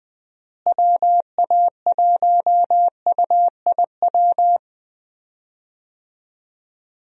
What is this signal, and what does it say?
Morse code, heard as a keyed tone, about 21 wpm, 700 Hz. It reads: WA1UIW
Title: USA Call Signs - 6 Characters - Repeated 3x - 20wpm